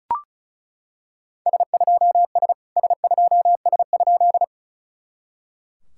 Morse code, heard as a keyed tone, about 36 wpm, 700 Hz. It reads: S2S S2S?
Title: SOTA QSO Practice 35wpm